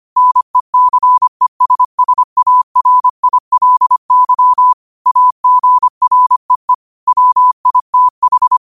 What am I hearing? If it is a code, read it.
Morse code, heard as a keyed tone, about 25 wpm, 1000 Hz. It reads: NECESSARILY AGREE WITH
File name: qotdfiles / 25wpm